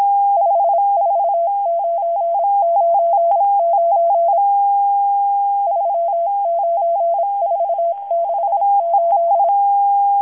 Morse code, heard as a keyed tone, about 26 wpm, 680 Hz. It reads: 54999 39468
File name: HM02- -0437utc-7351khz